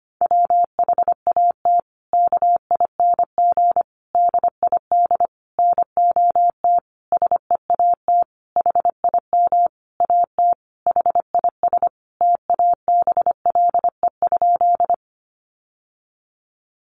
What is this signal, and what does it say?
Morse code, heard as a keyed tone, about 25 wpm, 700 Hz. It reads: WHAT KING DID NOT SEAT HIM AT HIS TABLE?
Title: Sentences from Top 1500 Words 25wpm